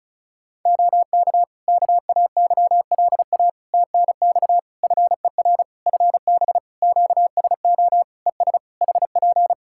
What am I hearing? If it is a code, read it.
Morse code, heard as a keyed tone, about 36 wpm, 700 Hz. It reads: OK KAYLA TNX FER FB QSO ES HP
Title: Practice QSO - Exchange 4 - Ending - 35wpm